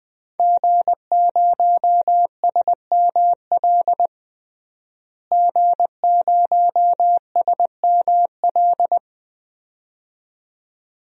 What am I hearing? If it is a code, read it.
Morse code, heard as a keyed tone, about 20 wpm, 700 Hz. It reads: G0SML G0SML
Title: DXCC-100 Call Signs - 5 Characters - Repeated 2x - 20wpm